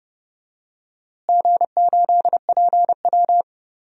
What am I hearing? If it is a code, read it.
Morse code, heard as a keyed tone, about 30 wpm, 700 Hz. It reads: G8PW
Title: DXCC-100 Call Signs - 4 Characters - Repeated 3x - 30wpm